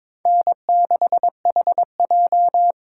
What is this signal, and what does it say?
Morse code, heard as a keyed tone, about 26 wpm, 700 Hz. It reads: N6HJ
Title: Speed Racing - USA 1x2 Call Signs 22wpm